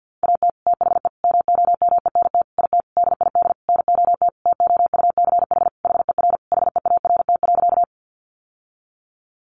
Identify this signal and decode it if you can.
Morse code, heard as a keyed tone, about 50 wpm, 700 Hz. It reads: AT THE MOMENT IT DID NOT TOUCH HER HEART.
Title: Sentences from Top 1200 Words 50wpm